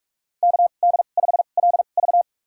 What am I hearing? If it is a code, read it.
Morse code, heard as a keyed tone, about 43 wpm, spaced as for 32 wpm, 700 Hz. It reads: KD5LV
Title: USA Call Signs - 5 Characters - 2x Character Spacing - 45wpm